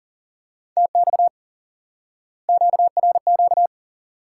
Morse code, heard as a keyed tone, about 40 wpm, 700 Hz. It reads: TX QRQ
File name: IQE Course 3 - Lesson 89 - YAGI - Two Elements - 40wpm